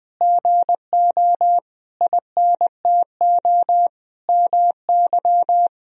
Morse code, heard as a keyed tone, about 20 wpm, 700 Hz. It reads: GO INTO MY
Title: Sentences from Top 1000 Words - Review Part 1 - 20wpm